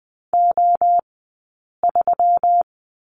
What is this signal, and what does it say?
Morse code, heard as a keyed tone, about 20 wpm, spaced as for 8 wpm, 700 Hz. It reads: O3
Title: 165 Lesson 27d - Number 7 - Two Characters 20wpm